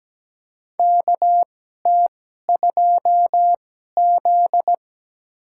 Morse code, heard as a keyed tone, about 17 wpm, spaced as for 12 wpm, 700 Hz. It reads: KT2Z
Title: USA Call Signs - 4 Characters - 2x Character Spacing - 17wpm